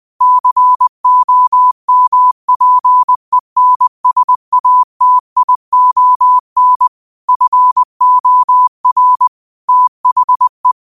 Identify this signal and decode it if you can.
Morse code, heard as a keyed tone, about 20 wpm, 1000 Hz. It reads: COMPENSATION FOR THE